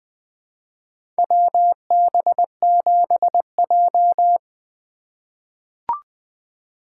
Morse code, heard as a keyed tone, about 20 wpm, 700 Hz. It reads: WB7J
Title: USA Call Signs - 4 Characters - 2x Character Spacing - 20wpm